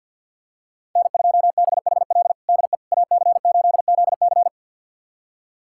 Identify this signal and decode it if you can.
Morse code, heard as a keyed tone, about 51 wpm, 700 Hz. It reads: N2BHL DE AC8ZC
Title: USA Callsign DE Callsign - 5 Characters - 50wpm